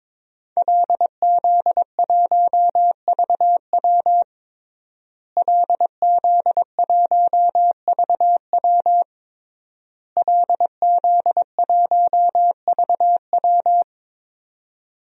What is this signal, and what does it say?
Morse code, heard as a keyed tone, about 22 wpm, 700 Hz. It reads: LZ1VW LZ1VW LZ1VW
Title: DXCC-100 Call Signs - 5 Characters - Repeated 3x - 22wpm